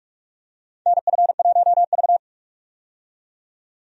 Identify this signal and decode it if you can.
Morse code, heard as a keyed tone, about 44 wpm, 700 Hz. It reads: NF1V
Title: USA Call Signs - 4 Characters - 45wpm